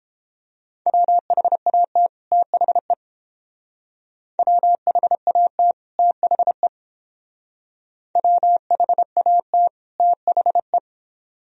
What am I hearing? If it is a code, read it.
Morse code, heard as a keyed tone, about 30 wpm, 700 Hz. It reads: WHAT THE WHAT THE WHAT THE